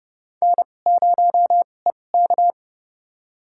Morse code, heard as a keyed tone, about 30 wpm, spaced as for 22 wpm, 700 Hz. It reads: N0EK
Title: USA Call Signs - 4 Characters - 2x Character Spacing - 30wpm